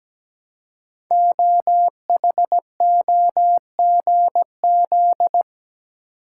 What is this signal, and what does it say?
Morse code, heard as a keyed tone, about 17 wpm, 700 Hz. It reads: OHOGZ
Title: Code Groups - 5 Letters - 17wpm